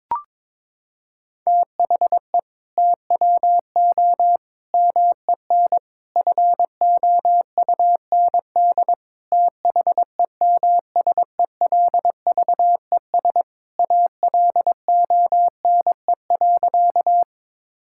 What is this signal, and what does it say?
Morse code, heard as a keyed tone, about 22 wpm, 700 Hz. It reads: THE TWO MEN FOUND THEMSELVES ALONE.
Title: Sentences from Top 1000 Words - Review Part 5 - 22wpm